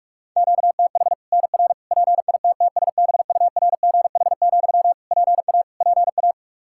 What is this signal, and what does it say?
Morse code, heard as a keyed tone, about 46 wpm, 700 Hz. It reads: QTH NR PITTSBURGH, PA PA
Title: Practice QSO - Exchange 1 - Beginning - 45wpm